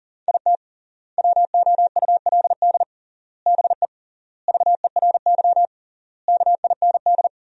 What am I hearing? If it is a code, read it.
Morse code, heard as a keyed tone, about 40 wpm, 700 Hz. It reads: IT WOULD BE VERY KIND
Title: 3x Word-Spacing - Sentences from Top 200 Words - 40wpm